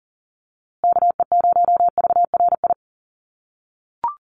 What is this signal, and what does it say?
Morse code, heard as a keyed tone, about 40 wpm, 700 Hz. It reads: KE0VRI
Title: USA Call Signs Repeated 2x 40wpm